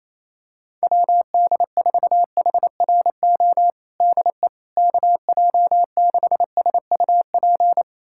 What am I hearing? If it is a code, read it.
Morse code, heard as a keyed tone, about 28 wpm, 700 Hz. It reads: WD4HRO DE KJ6SUP
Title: USA Callsign DE Callsign 28wpm